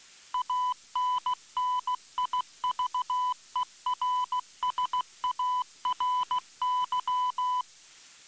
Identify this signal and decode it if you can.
Morse code, heard as a keyed tone, about 16 wpm, 1000 Hz. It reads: ANNIVERSARY